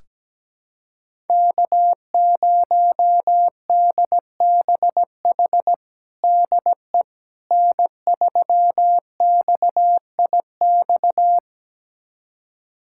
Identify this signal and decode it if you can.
Morse code, heard as a keyed tone, about 17 wpm, 700 Hz. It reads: K0DBH DE N3XIX
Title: USA Callsign DE Callsign - 5 Characters - 17wpm